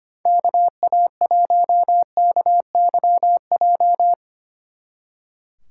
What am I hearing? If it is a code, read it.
Morse code, heard as a keyed tone, about 30 wpm, 700 Hz. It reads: KA1KYJ
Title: Speed Racing - USA Call Signs 25wpm